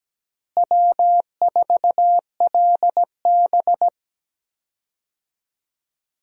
Morse code, heard as a keyed tone, about 17 wpm, 700 Hz. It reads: W4LB
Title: USA Call Signs - 4 Characters - Repeated 2x - 17wpm